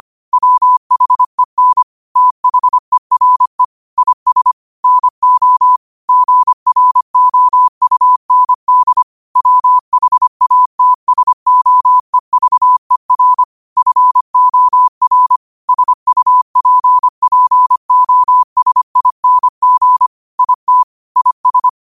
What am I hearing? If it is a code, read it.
Morse code, heard as a keyed tone, about 25 wpm, 1000 Hz. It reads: WHEN THERE IS NO GROUND WHATSOEVER FOR SUPPOSING IT IS